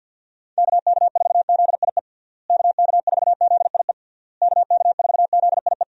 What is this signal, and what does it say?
Morse code, heard as a keyed tone, about 51 wpm, 700 Hz. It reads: KK4ZIE KK4ZIE KK4ZIE
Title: USA Call Signs - 6 Characters - Repeated 3x - 50wpm